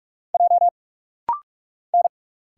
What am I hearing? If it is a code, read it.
Morse code, heard as a keyed tone, about 46 wpm, 700 Hz. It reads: J N
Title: Single Letter-Number - Rapid-Fire 45wpm